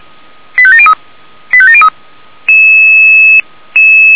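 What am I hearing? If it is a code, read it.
Five-tone selective call (ZVEI), heard as quick runs of five tones, tones of about 70 ms. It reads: 85703 85703
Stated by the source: ZVEI five-tone call 98575 received by SDR.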